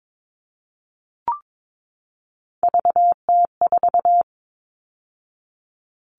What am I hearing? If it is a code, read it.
Morse code, heard as a keyed tone, about 22 wpm, 700 Hz. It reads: VT4
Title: All Cut Numbers - 3-Digit 22wpm